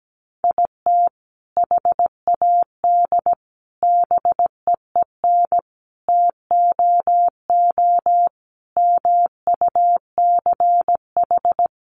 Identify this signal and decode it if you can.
Morse code, heard as a keyed tone, about 17 wpm, 700 Hz. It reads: IT HAD BEEN TOO MUCH
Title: Sentences from Top 1300 Words 17wpm